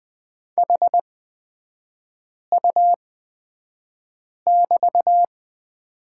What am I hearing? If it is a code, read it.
Morse code, heard as a keyed tone, about 20 wpm, spaced as for 5 wpm, 700 Hz. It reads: HU<BT>
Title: 250 Lesson 40e - BK - Three Characters 20wpm